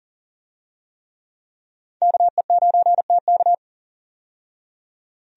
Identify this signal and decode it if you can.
Morse code, heard as a keyed tone, about 40 wpm, 700 Hz. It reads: KE9TK
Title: USA Call Signs - 5 Characters - 40wpm